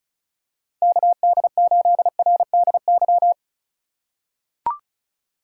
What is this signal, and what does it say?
Morse code, heard as a keyed tone, about 34 wpm, 700 Hz. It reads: KD8RDY
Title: USA Call Signs - 6 Characters - 4x Character Spacing - 35wpm